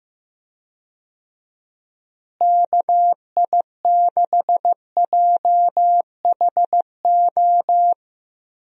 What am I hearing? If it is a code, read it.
Morse code, heard as a keyed tone, about 15 wpm, 700 Hz. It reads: KI6JHO
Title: USA Call Signs - 6 Characters - Repeated 2x - 15wpm